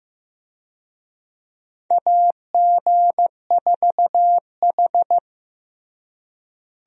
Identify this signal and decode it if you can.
Morse code, heard as a keyed tone, about 15 wpm, 700 Hz. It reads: AG4H
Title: USA Call Signs - 4 Characters - Repeated 2x - 15wpm